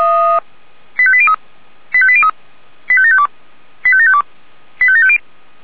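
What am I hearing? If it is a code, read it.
Five-tone selective call (ZVEI), heard as quick runs of five tones, tones of about 70 ms. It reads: 85803 85803 85752 85752 85750
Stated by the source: ZVEI five-tone selective call received by SDR.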